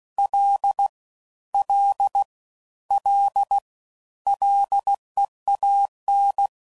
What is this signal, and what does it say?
Morse code, heard as a keyed tone, about 16 wpm, 800 Hz. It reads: L L L LEAN